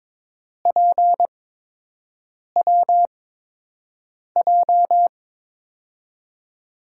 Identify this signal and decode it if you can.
Morse code, heard as a keyed tone, about 22 wpm, spaced as for 6 wpm, 700 Hz. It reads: PWJ
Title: USA Call Signs - 5 Characters - 8x Character Spacing - 22wpm